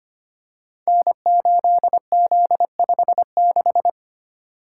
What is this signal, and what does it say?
Morse code, heard as a keyed tone, about 25 wpm, 700 Hz. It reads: N8Z56
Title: Code Groups - 5 Letter:Numbers - 25wpm